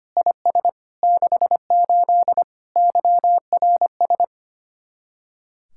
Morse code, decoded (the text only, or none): IS 68 YRS